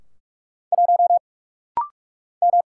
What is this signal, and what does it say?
Morse code, heard as a keyed tone, about 46 wpm, 700 Hz. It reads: 1 M
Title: Single Letter-Number Alt - Rapid-Fire 45wpm